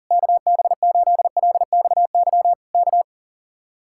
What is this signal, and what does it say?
Morse code, heard as a keyed tone, about 37 wpm, 700 Hz. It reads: KB8LXY K
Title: Speed-Racing Common QSO Phrases Encore - 40wpm